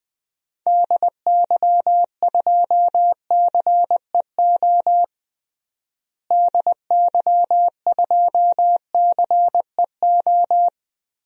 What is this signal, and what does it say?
Morse code, heard as a keyed tone, about 20 wpm, 700 Hz. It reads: DY2CEO DY2CEO
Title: DXCC-100 Call Signs - 6 Characters - Repeated 3x - 20wpm